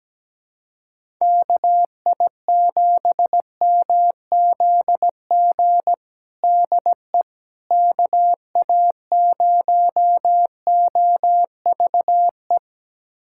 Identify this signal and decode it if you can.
Morse code, heard as a keyed tone, about 17 wpm, 700 Hz. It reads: KI7MZG DE KA0OVE
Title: USA Callsign DE Callsign 17wpm